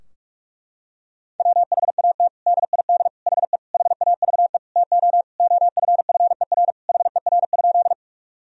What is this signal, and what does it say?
Morse code, heard as a keyed tone, about 41 wpm, 700 Hz. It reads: WHAT DID HE HAVE TO OFFER HER?
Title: Sentences from Top 1000 Words - Review Part 4 - 45wpm